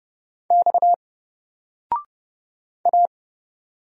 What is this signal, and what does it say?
Morse code, heard as a keyed tone, about 30 wpm, 700 Hz. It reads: X A
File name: Single-Letter-Number-30wpm